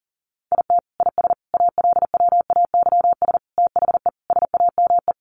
Morse code, heard as a keyed tone, about 37 wpm, 700 Hz. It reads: IT IS ALWAYS THE SAME
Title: Sentences from Top 1400 Words 40wpm